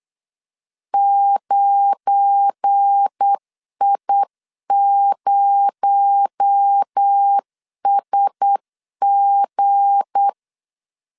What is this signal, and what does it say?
Morse code, heard as a keyed tone, about 8 wpm, 800 Hz. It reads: 9I0SG
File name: CW / DET / 1 9 1500C S A 8wpm 5637